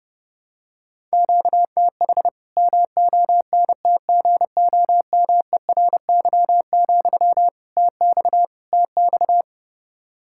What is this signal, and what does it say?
Morse code, heard as a keyed tone, about 30 wpm, 700 Hz. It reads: QTH MONTGOMERY, TX TX
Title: Practice QSO - Exchange 1 - Beginning - 30wpm